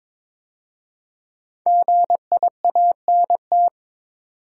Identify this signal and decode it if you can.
Morse code, heard as a keyed tone, about 22 wpm, 700 Hz. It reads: GIANT
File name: 5 Letter Words - First 3 Letters Given - 22wpm